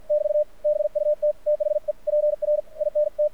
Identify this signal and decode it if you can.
Morse code, heard as a keyed tone, about 51 wpm, spaced as for 43 wpm, 600 Hz, with radio noise. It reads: <BT> BUT THE PRINT
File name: archived / CRSnet